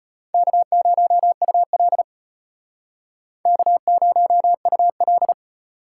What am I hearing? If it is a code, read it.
Morse code, heard as a keyed tone, about 38 wpm, 700 Hz. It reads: K0UL K0UL